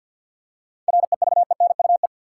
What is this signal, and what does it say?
Morse code, heard as a keyed tone, about 50 wpm, 700 Hz. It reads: REVENUE
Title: Seven Letter Words - 50wpm